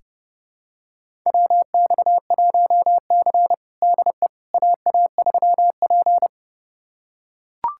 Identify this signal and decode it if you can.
Morse code, heard as a keyed tone, about 32 wpm, 700 Hz. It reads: WX1C DE AA3P
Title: USA Callsign DE Callsign - 4 Characters - 30wpm